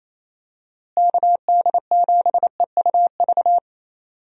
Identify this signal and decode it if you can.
Morse code, heard as a keyed tone, about 27 wpm, 700 Hz. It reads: KD7EUV
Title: USA Call Signs - 6 Characters - Repeated 3x - 28wpm